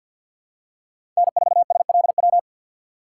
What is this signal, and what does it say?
Morse code, heard as a keyed tone, about 52 wpm, 700 Hz. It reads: N4SLW
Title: USA Call Signs - 6x Character Spacing - 50wpm